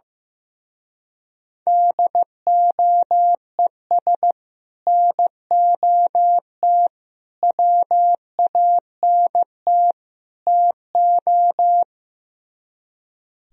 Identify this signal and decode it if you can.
Morse code, heard as a keyed tone, about 15 wpm, 700 Hz. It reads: DOES NOT WANT TO
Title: Speed-Racing - Sets of 4 Words 15wpm